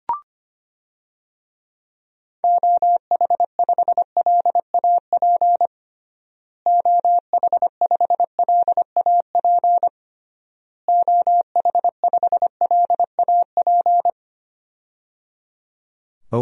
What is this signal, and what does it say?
Morse code, heard as a keyed tone, about 25 wpm, 700 Hz. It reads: OH5LAP OH5LAP OH5LAP
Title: DXCC-100 Call Signs - 6 Characters - Repeated 3x - 25wpm